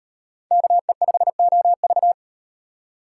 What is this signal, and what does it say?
Morse code, heard as a keyed tone, about 35 wpm, 700 Hz. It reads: KE5OV